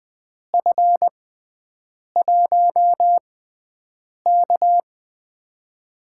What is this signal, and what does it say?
Morse code, heard as a keyed tone, about 20 wpm, spaced as for 7 wpm, 700 Hz. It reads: F1K
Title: USA Call Signs - 4 Characters - 6x Character Spacing - 20wpm